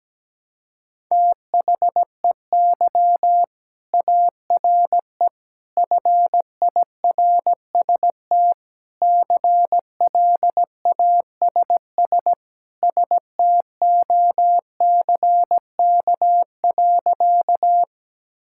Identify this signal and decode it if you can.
Morse code, heard as a keyed tone, about 17 wpm, 700 Hz. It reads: THEY ARE FIRST CLASS STOCK.
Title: Sentences from Top 1000 Words - Review Part 5 - 17wpm